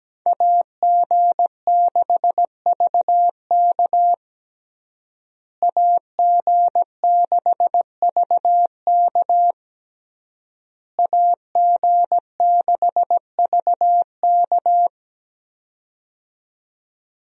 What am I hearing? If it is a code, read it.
Morse code, heard as a keyed tone, about 17 wpm, 700 Hz. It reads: AG6VK AG6VK AG6VK
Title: USA Call Signs - 5 Characters - Repeated 3x - 17wpm